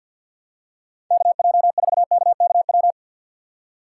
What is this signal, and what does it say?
Morse code, heard as a keyed tone, about 48 wpm, 700 Hz. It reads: KJ4KKW
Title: USA Call Signs - 6 Characters - 6x Character Spacing - 50wpm